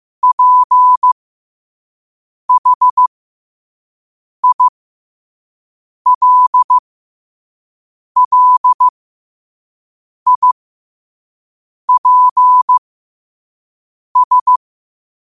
Morse code, heard as a keyed tone, about 15 wpm, spaced as for 5 wpm, 1000 Hz. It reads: PHILLIPS
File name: qotdfiles / 5wpm